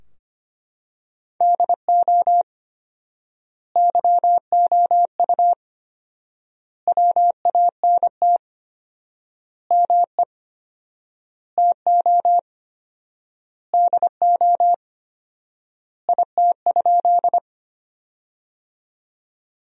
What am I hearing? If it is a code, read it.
Morse code, heard as a keyed tone, about 25 wpm, 700 Hz. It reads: DO YOU WANT ME TO DO IT?
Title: 4x Word-Spacing - Sentences from Top 200 Words - 25wpm